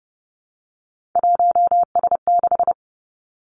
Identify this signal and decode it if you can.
Morse code, heard as a keyed tone, about 30 wpm, 700 Hz. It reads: 1S6
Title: Three Challenging Letter-Number Combinations 30wpm